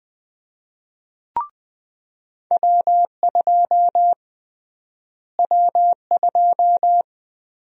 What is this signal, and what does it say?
Morse code, heard as a keyed tone, about 20 wpm, 700 Hz. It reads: W2 W2
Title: USA Call Sign Prefixes - Repeated 2x - 20wpm